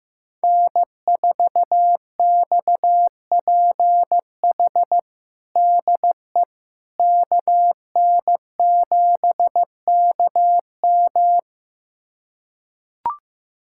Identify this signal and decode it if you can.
Morse code, heard as a keyed tone, about 15 wpm, 700 Hz. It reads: N4XPH DE KN7KM
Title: USA Callsign DE Callsign - 5 Characters - 15wpm